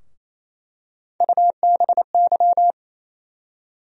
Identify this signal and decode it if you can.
Morse code, heard as a keyed tone, about 28 wpm, 700 Hz. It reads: UBY
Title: USA Call Sign Postfixes - 3 Letters 28wpm